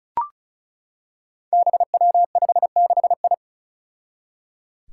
Morse code, heard as a keyed tone, about 36 wpm, 700 Hz. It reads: BW56I
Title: Five Challenging Letter-Number Combinations 35wpm